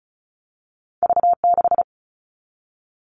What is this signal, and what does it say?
Morse code, heard as a keyed tone, about 36 wpm, 700 Hz. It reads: V6
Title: All Cut Numbers - 2-Digit 35wpm